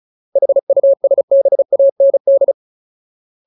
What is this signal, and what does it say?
Morse code, heard as a keyed tone, about 34 wpm, 550 Hz. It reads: HUSBAND
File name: Ultimate Word List / 35WPM Ultimate